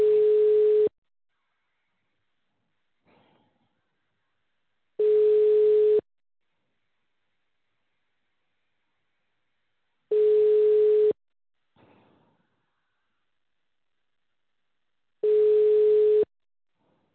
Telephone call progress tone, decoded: ringback tone